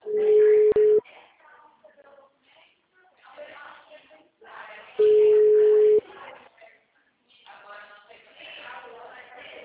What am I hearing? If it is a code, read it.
Ringback tone (Europe, 425 Hz), with speech.
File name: out- -2011- - - .57622